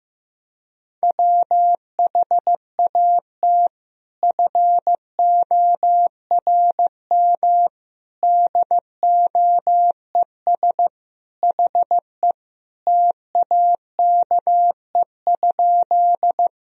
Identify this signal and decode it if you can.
Morse code, heard as a keyed tone, about 15 wpm, 700 Hz. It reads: WHAT FORM DOES HE TAKE?
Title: Sentences from Top 1000 Words - Review Part 1 - 15wpm